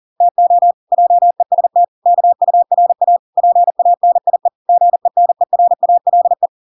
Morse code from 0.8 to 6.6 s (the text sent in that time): JEST KURA PANIE GENERALE